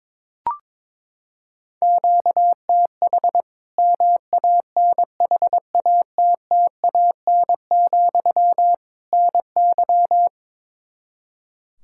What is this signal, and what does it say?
Morse code, heard as a keyed tone, about 21 wpm, 700 Hz. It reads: QTH MANHATTAN, NY
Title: Practice QSO - Exchange 1 - Beginning - 22wpm